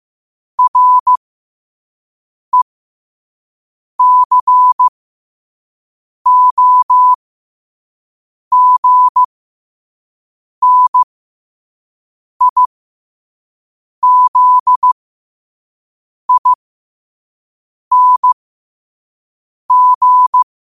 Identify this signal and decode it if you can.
Morse code, heard as a keyed tone, about 15 wpm, spaced as for 5 wpm, 1000 Hz. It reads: RECOGNIZING
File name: qotdfiles / 5wpm